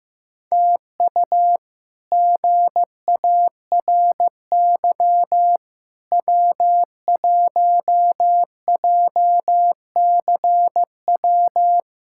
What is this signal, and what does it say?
Morse code, heard as a keyed tone, about 15 wpm, 700 Hz. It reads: TU GARY W1JCW
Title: K1USN SST QSO Practice - 15wpm